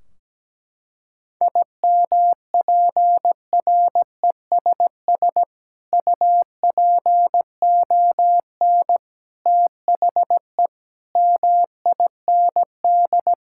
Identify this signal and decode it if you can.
Morse code, heard as a keyed tone, about 17 wpm, 700 Hz. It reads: IMPRESS UPON THE MIND
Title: Sentences from Top 1600 Words 17wpm